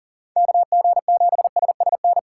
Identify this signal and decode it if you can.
Morse code, heard as a keyed tone, about 40 wpm, 700 Hz. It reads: KG7SSN
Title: USA Call Signs - 6x Character Spacing - 40wpm